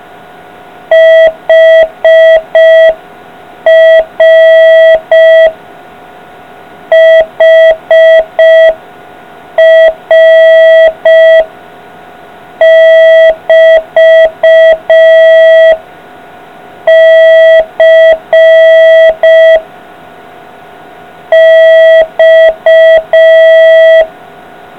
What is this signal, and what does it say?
Morse code, heard as a keyed tone, about 5 wpm, 650 Hz. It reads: HR HR <BT> C X